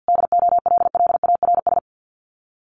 Morse code, heard as a keyed tone, about 50 wpm, 700 Hz. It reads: DOLLARS
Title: Sets of 5 Words Encore 50wpm